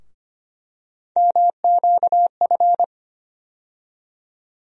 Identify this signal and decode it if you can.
Morse code, heard as a keyed tone, about 25 wpm, 700 Hz. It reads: MQF
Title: USA Call Sign Postfixes - 3 Letters 25wpm